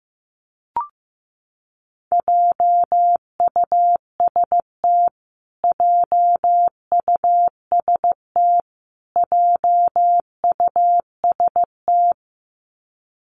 Morse code, heard as a keyed tone, about 15 wpm, 700 Hz. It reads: JUST JUST JUST
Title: Top 500 Words Repeated 15wpm